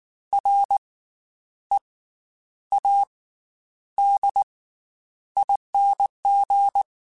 Morse code, heard as a keyed tone, about 19 wpm, 800 Hz. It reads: R E A D ING
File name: Special INGs / Special INGs-mp3s / Slow ing @18